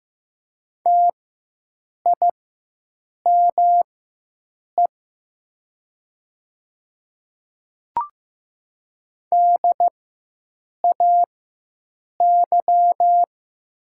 Morse code, heard as a keyed tone, about 15 wpm, spaced as for 7 wpm, 700 Hz. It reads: TIME DAY
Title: 4x Character Spacing - Top 100 Words - 15wpm